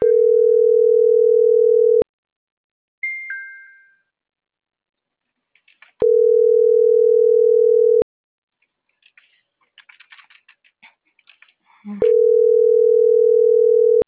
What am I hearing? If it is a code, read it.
Ringback tone (North America), 440 + 480 Hz.